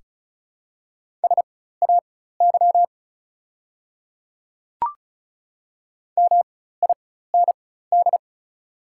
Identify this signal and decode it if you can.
Morse code, heard as a keyed tone, about 35 wpm, spaced as for 16 wpm, 690 Hz. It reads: SAY MIND
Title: 4x Character Spacing - Top 500 Words - 35wpm